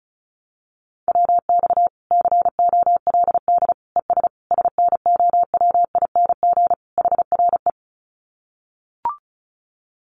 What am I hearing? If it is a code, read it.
Morse code, heard as a keyed tone, about 34 wpm, 700 Hz. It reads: WX COLD ES SNOWING HRE
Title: Common QSO Phrases 35wpm